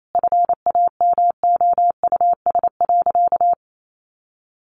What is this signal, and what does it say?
Morse code, heard as a keyed tone, about 28 wpm, 700 Hz. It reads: FAMOUS.
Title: Sentences from Top 1500 Words 28wpm